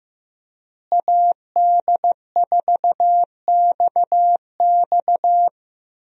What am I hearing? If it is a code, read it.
Morse code, heard as a keyed tone, about 14 wpm, 700 Hz. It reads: AD4XX
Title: USA Call Signs - 5 Characters - Repeated 3x - 15wpm